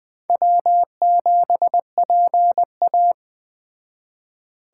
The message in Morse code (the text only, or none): W7PA